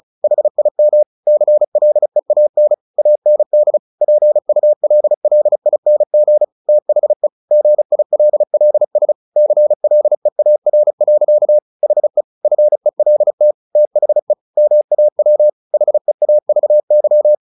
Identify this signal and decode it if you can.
Morse code, heard as a keyed tone, about 34 wpm, 600 Hz. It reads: HIM CLEAN AND PULLING THE GILLS CLEAR. HE FELT THE MAW HEAVY
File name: the old man and the sea 2 35wpm